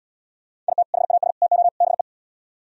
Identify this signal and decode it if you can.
Morse code, heard as a keyed tone, about 77 wpm, 700 Hz. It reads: IT HAS TO BE